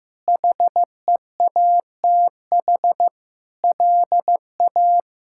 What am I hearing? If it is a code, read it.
Morse code, heard as a keyed tone, about 15 wpm, 700 Hz. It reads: HEATH LA
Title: K1USN SST QSO Practice - 15wpm